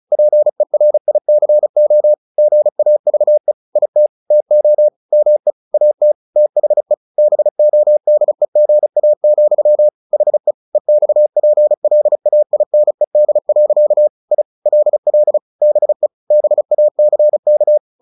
Morse code, heard as a keyed tone, about 34 wpm, 600 Hz. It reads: PERICO GAVE IT TO ME AT THE BODEGA, HE EXPLAINED. I LL BE BACK
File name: the old man and the sea / the old man and the sea 1 35wpm